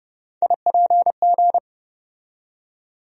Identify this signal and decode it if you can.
Morse code, heard as a keyed tone, about 30 wpm, 700 Hz. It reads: IPG
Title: USA Call Sign Postfixes - 3 Letters 30wpm